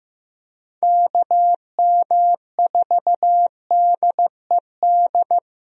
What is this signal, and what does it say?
Morse code, heard as a keyed tone, about 15 wpm, 700 Hz. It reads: KM4DED
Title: USA Call Signs - 6 Characters - Repeated 3x - 15wpm